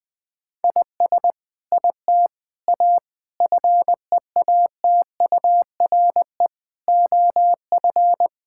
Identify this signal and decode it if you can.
Morse code, heard as a keyed tone, about 20 wpm, 700 Hz. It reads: IS IT A FEATURE OF
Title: Sentences from Top 1000 Words - Review Part 5 - 20wpm